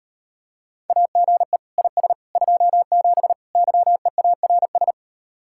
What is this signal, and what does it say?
Morse code, heard as a keyed tone, about 38 wpm, 700 Hz. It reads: AGE IS 27 YEARS